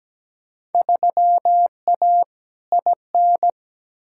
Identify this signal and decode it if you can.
Morse code, heard as a keyed tone, about 17 wpm, 700 Hz. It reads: 3A IN
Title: Field Day Exchanges - 17wpm